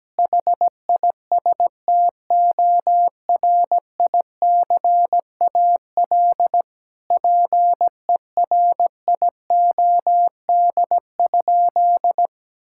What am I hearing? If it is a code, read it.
Morse code, heard as a keyed tone, about 17 wpm, 700 Hz. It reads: HISTORICAL PERIOD?
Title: Sentences from Top 1000 Words - Review Part 5 - 17wpm